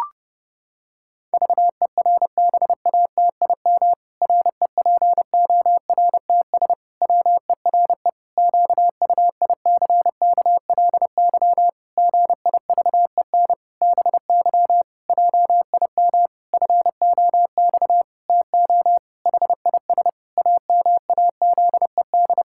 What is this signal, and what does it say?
Morse code, heard as a keyed tone, about 32 wpm, 700 Hz. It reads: VERBATIM REPORTS WERE QUICKLY GIVEN BY JIM FOX TO HIS AMAZED
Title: Pangram Challenge v2 - 30wpm